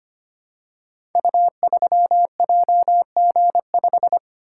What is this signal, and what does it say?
Morse code, heard as a keyed tone, about 25 wpm, 700 Hz. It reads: U3JG5
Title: Code Groups - 5 Letter:Numbers - 25wpm